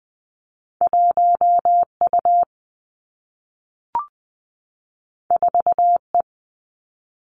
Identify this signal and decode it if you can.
Morse code, heard as a keyed tone, about 20 wpm, 700 Hz. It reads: 1 U 4 E
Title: Two Challenging Letter-Number Combinations 20wpm